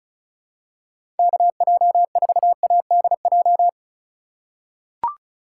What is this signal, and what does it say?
Morse code, heard as a keyed tone, about 36 wpm, 700 Hz. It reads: KJ4ADJ
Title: USA Call Signs - 6 Characters - 3x Character Spacing - 35wpm